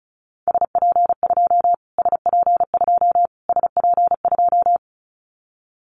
Morse code, heard as a keyed tone, about 34 wpm, 700 Hz. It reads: SP2 SP2 SP2
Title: Callsign Prefixes for Top 25 DXCC - Repeated 3x - 35wpm